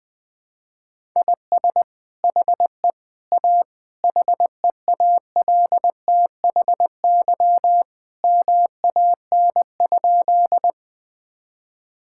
Morse code, heard as a keyed tone, about 21 wpm, 700 Hz. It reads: IS HE A HEALTHY MAN?
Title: Sentences from Top 1000 Words - Review Part 5 - 20wpm